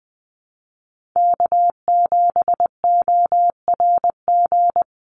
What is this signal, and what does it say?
Morse code, heard as a keyed tone, about 20 wpm, 690 Hz. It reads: K7ORG
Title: NA SOTA Chaser Callsigns 20wpm